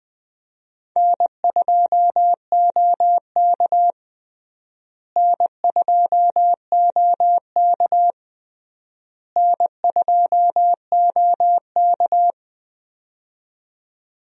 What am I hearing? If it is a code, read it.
Morse code, heard as a keyed tone, about 21 wpm, 700 Hz. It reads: N2OK N2OK N2OK
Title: USA Call Signs - 4 Characters - Repeated 3x - 20wpm